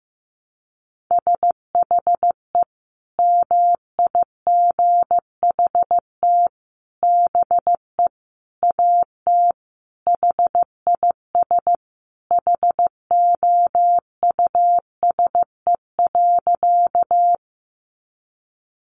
Morse code, heard as a keyed tone, about 15 wpm, 700 Hz. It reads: SHE MIGHT BE AT HIS HOUSE.
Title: Sentences from Top 300 Words 15wpm